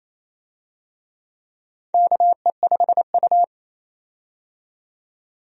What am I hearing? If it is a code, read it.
Morse code, heard as a keyed tone, about 27 wpm, 700 Hz. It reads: KE5U
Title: USA Call Signs - 4 Characters - 28wpm